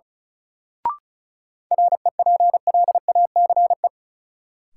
Morse code, heard as a keyed tone, about 36 wpm, 700 Hz. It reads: REPLACE
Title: Sentences from Top 1400 Words - New Word Drill - 35wpm